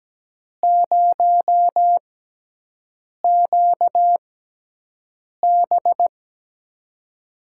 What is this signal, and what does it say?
Morse code, heard as a keyed tone, about 17 wpm, spaced as for 6 wpm, 700 Hz. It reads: 0QB
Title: USA Call Signs - 5 Characters - 6x Character Spacing - 17wpm